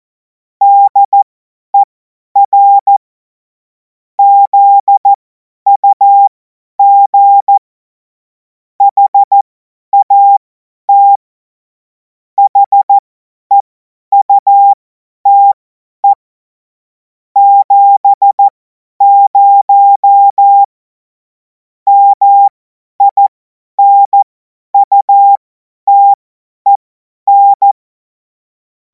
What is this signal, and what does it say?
Morse code, heard as a keyed tone, about 14 wpm, spaced as for 10 wpm, 800 Hz. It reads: DER ZUG HAT HEUTE 70 MINUTEN